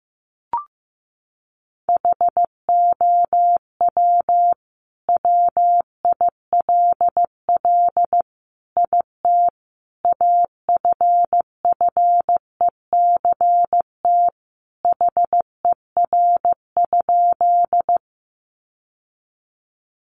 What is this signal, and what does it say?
Morse code, heard as a keyed tone, about 15 wpm, 700 Hz. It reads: HOW WILL IT AFFECT HER?
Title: Sentences from Top 900 Words 15wpm